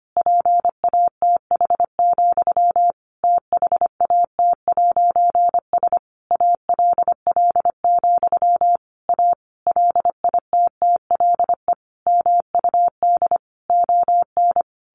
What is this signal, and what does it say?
Morse code, heard as a keyed tone, about 25 wpm, 700 Hz. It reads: PATH, THAT'S ALL, A LITTLE MUD ON
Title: A Sound Of Thunder 25wpm